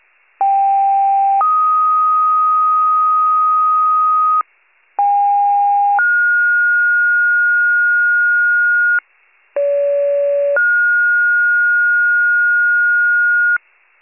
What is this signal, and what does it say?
Two-tone page, heard as a one-second tone, then a three-second tone, about 790 then 1250 Hz, about 810 then 1430 Hz, about 570 then 1430 Hz, with noise. It is heard from 0.4 to 13.6 s.